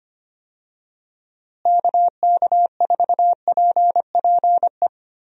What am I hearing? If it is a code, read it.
Morse code, heard as a keyed tone, about 25 wpm, 700 Hz. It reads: KK4PPE
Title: USA Call Signs - 6 Characters - Repeated 3x - 25wpm